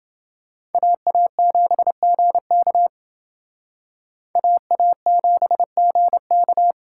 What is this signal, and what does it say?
Morse code, heard as a keyed tone, about 28 wpm, 700 Hz. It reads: AA7GK AA7GK